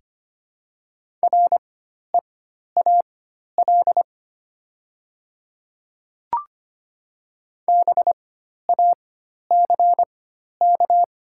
Morse code, heard as a keyed tone, about 25 wpm, spaced as for 12 wpm, 700 Hz. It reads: REAL BACK
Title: 4x Character Spacing - Top 500 Words - 25wpm